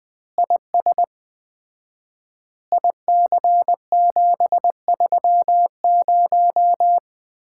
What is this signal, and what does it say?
Morse code, heard as a keyed tone, about 21 wpm, 700 Hz. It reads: IS IC730
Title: Common QSO Phrases - 4x Word-Spacing - 20wpm